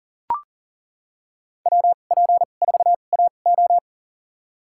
Morse code, heard as a keyed tone, about 40 wpm, spaced as for 29 wpm, 700 Hz. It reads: WP4AO
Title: USA Call Signs - 5 Characters - 2x Character Spacing - 40wpm